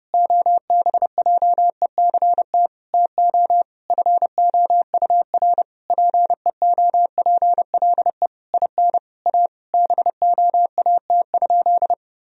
Morse code, heard as a keyed tone, about 30 wpm, 700 Hz. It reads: OBJECT TO FOUR PEOPLE IN A BOAT?
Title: Sentences from Top 1000 Words - Review Part 3 - 30wpm